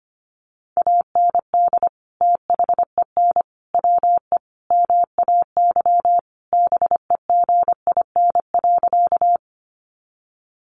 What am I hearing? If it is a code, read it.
Morse code, heard as a keyed tone, about 25 wpm, 700 Hz. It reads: AND THEN WE MAY BEGIN.
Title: Sentences from Top 300 Words 25wpm